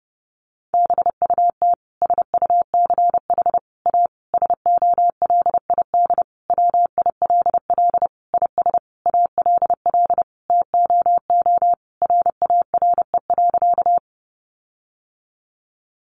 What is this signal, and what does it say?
Morse code, heard as a keyed tone, about 30 wpm, 700 Hz. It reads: BUT SUCH A SOLID WILL IS ALL TOO RARE.
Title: Sentences from Top 1500 Words 30wpm